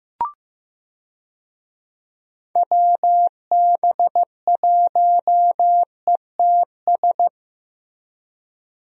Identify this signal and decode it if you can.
Morse code, heard as a keyed tone, about 15 wpm, 700 Hz. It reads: WB1ETS
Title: USA Call Signs - 6 Characters - Repeated 2x - 15wpm